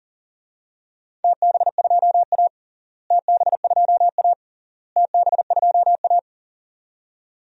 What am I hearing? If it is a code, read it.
Morse code, heard as a keyed tone, about 40 wpm, 700 Hz. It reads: TB2A TB2A TB2A
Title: DXCC-100 Call Signs - 4 Characters - Repeated 3x - 40wpm